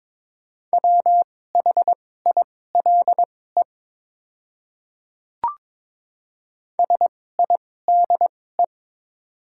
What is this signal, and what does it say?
Morse code, heard as a keyed tone, about 21 wpm, spaced as for 16 wpm, 700 Hz. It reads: WHILE SIDE
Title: 2x Character Spacing - Top 300 Words - 22wpm